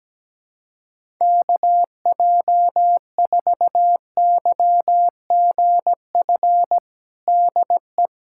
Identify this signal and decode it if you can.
Morse code, heard as a keyed tone, about 17 wpm, 700 Hz. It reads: KJ4YGF DE
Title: USA Callsign DE Callsign 17wpm